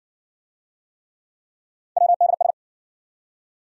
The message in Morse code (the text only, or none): WDS